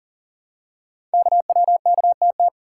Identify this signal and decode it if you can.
Morse code, heard as a keyed tone, about 40 wpm, 700 Hz. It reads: KWKTT
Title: Code Groups - 5 Letters - 40wpm